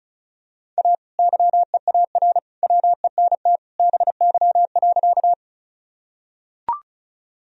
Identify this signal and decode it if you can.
Morse code, heard as a keyed tone, about 34 wpm, 700 Hz. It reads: A YEAR WENT BY.
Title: Sentences from Top 1000 Words - Review Part 1 - 35wpm